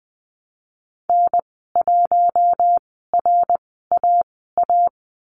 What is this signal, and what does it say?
Morse code, heard as a keyed tone, about 20 wpm, spaced as for 15 wpm, 700 Hz. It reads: N1RAA
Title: 062 Lesson 11g - Letter h - Call Signs 20wpm